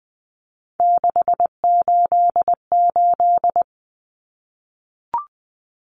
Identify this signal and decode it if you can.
Morse code, heard as a keyed tone, about 20 wpm, 700 Hz. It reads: 688
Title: Numbers - 3-Digit 20wpm